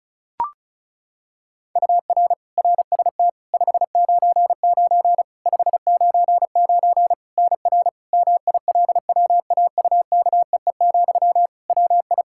Morse code, heard as a keyed tone, about 34 wpm, 700 Hz. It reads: UR RST 599 599 NR MILWAUKEE, WI
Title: Practice QSO - Exchange 1 - Beginning - 35wpm